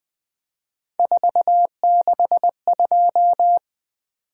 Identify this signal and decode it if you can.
Morse code, heard as a keyed tone, about 20 wpm, 700 Hz. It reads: 462
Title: Three Letter-Number Combinations - 20wpm